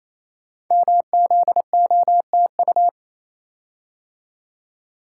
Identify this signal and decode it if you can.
Morse code, heard as a keyed tone, about 28 wpm, 700 Hz. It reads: MZOTU
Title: Code Groups - 5 Letters - 28wpm